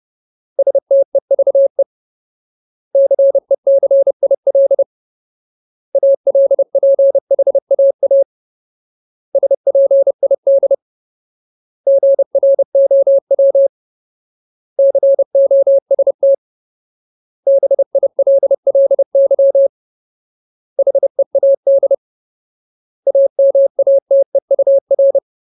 Morse code, heard as a keyed tone, about 30 wpm, 550 Hz. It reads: STEVE CECIL ALPHAA SPID GROW COST BILLY HEAD AMATEUR
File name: Ultimate Word List / 30WPM Ultimate